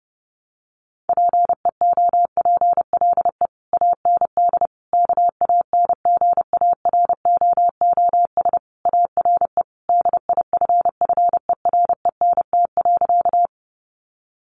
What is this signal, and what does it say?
Morse code, heard as a keyed tone, about 30 wpm, 700 Hz. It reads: PEOPLE AND KANGAROOS ARE DIFFERENT.
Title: Easy Phrases 30wpm